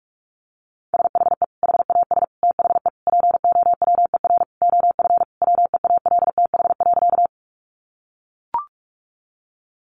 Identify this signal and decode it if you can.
Morse code, heard as a keyed tone, about 44 wpm, 700 Hz. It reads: SHE HAS THE POWER OF WEALTH.
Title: Sentences from Top 1200 Words 45wpm